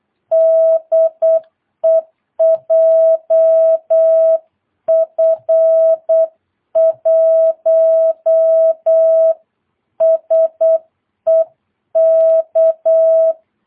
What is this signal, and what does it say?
Morse code, heard as a keyed tone, about 8 wpm, 650 Hz. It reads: DE JF1SEK
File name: jh8cbh / 2022cw